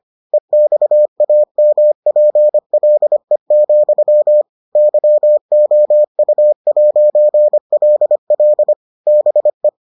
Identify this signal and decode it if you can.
Morse code, heard as a keyed tone, about 25 wpm, 600 Hz. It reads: EXAMPLE, YOU'LL BE